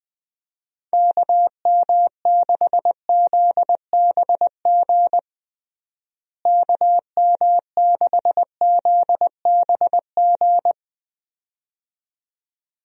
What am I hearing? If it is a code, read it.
Morse code, heard as a keyed tone, about 20 wpm, 700 Hz. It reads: KM6ZBG KM6ZBG
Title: USA Call Signs - 6 Characters - Repeated 2x - 20wpm